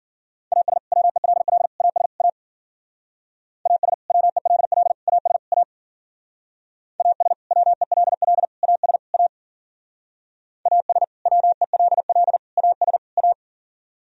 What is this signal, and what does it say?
Morse code, heard as a keyed tone, about 49 wpm, 700 Hz. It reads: AS WELL AS A AS WELL AS A AS WELL AS A AS WELL AS A
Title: Speed-Racing - Sets of 4 Words 40wpm